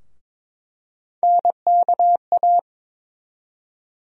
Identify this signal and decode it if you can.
Morse code, heard as a keyed tone, about 21 wpm, 700 Hz. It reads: NKA
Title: USA Call Sign Postfixes - 3 Letters 22wpm